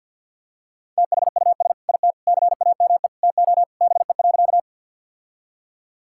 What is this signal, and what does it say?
Morse code, heard as a keyed tone, about 50 wpm, 700 Hz. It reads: THUS IT CAME TO BE.
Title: Sentences from Top 1000 Words - Review Part 5 - 50wpm